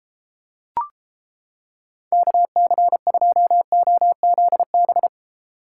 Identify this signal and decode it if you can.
Morse code, heard as a keyed tone, about 29 wpm, 700 Hz. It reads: KC2OZB